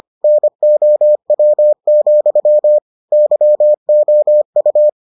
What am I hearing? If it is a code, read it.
Morse code, heard as a keyed tone, about 25 wpm, 600 Hz. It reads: NOW, YOU